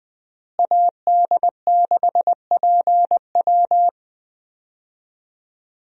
Morse code, heard as a keyed tone, about 20 wpm, 700 Hz. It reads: AD6PW
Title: USA Call Signs - 5 Characters - Repeated 2x - 20wpm